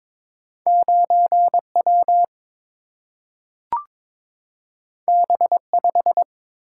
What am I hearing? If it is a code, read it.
Morse code, heard as a keyed tone, about 22 wpm, 700 Hz. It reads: 9 W B 5
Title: Single-Letter Single-Number 22wpm